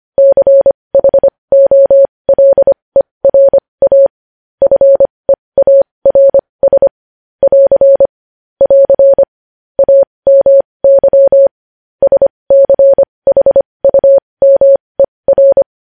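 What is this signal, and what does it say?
Morse code, heard as a keyed tone, about 25 wpm, spaced as for 20 wpm, 550 Hz. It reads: CHOLERA FEARS <AR> <AR> AMY SCHUMER
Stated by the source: Morse code news headlines sent at 20.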